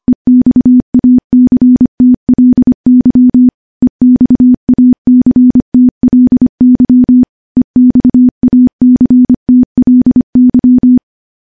Morse code, heard as a keyed tone, about 25 wpm, 250 Hz. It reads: EXACTLY EXACTLY EXACTLY
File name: morse / morse-training-25WPM